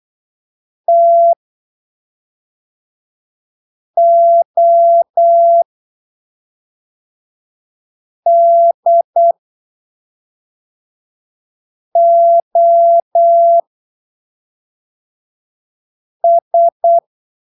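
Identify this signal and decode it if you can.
Morse code, heard as a keyed tone, about 8 wpm, spaced as for 3 wpm, 680 Hz. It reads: TODOS
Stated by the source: Morse training text at 8 ppm for Brazilian class B amateur radio telegraphy exams, from Portal do CW, PY2BRL.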